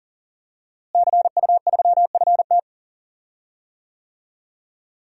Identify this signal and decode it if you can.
Morse code, heard as a keyed tone, about 40 wpm, 700 Hz. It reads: CU3FT
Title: DXCC-100 Call Signs - 5 Characters - Repeated 3x - 40wpm